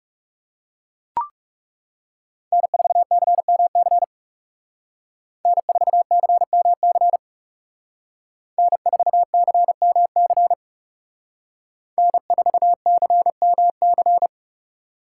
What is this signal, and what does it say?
Morse code, heard as a keyed tone, about 34 wpm, 700 Hz. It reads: N4CMC N4CMC N4CMC N4CMC
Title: Speed Racing - USA 1x3 Call Signs 30wpm